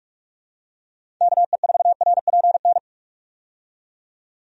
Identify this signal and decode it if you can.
Morse code, heard as a keyed tone, about 46 wpm, 700 Hz. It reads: KE4RPN
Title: USA Call Signs - 6 Characters - 45wpm